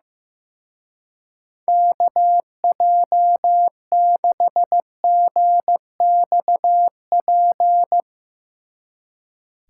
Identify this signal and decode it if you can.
Morse code, heard as a keyed tone, about 15 wpm, 700 Hz. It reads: KJ6GXP
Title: Speed Racing - USA 2x3 Call Signs 15wpm